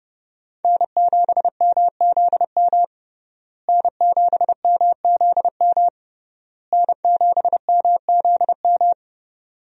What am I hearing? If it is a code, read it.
Morse code, heard as a keyed tone, about 30 wpm, 700 Hz. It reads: N7MZM N7MZM N7MZM
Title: USA Call Signs - 5 Characters - Repeated 3x - 30wpm